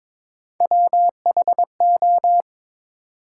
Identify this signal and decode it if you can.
Morse code, heard as a keyed tone, about 22 wpm, 700 Hz. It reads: WHO
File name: Build A Phrase - 2-Word Phrases - 22wpm